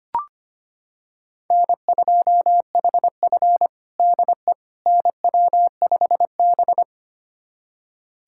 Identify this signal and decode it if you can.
Morse code, heard as a keyed tone, about 25 wpm, 700 Hz. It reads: N2HF DE NW5B
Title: USA Callsign DE Callsign - 4 Characters - 25wpm